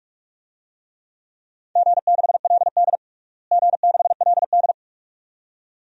Morse code, heard as a keyed tone, about 45 wpm, 700 Hz. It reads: G6LD G6LD
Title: DXCC-100 Call Signs - 4 Characters - Repeated 2x - 45wpm